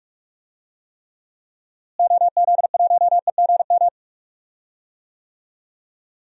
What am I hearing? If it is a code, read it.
Morse code, heard as a keyed tone, about 46 wpm, 700 Hz. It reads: OZ1EGM
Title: DXCC-100 Call Signs - 6 Characters - Repeated 3x - 45wpm